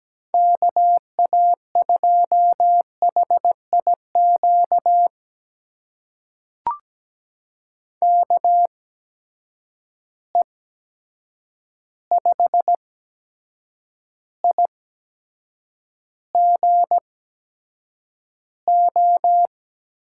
Morse code, heard as a keyed tone, about 17 wpm, spaced as for 7 wpm, 700 Hz. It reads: KA2HIQ KE5IGO
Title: USA Call Signs - 6 Characters - 8x Character Spacing - 17wpm